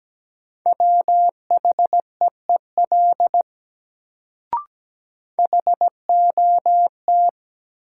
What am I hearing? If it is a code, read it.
Morse code, heard as a keyed tone, about 17 wpm, 700 Hz. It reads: WHEEL HOT
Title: Sentences from Top 500 Words - New Word Drill - 17wpm